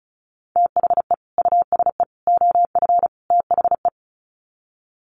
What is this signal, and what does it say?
Morse code, heard as a keyed tone, about 34 wpm, 700 Hz. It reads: THE USE OF THE
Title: Sets of 4 Words 35wpm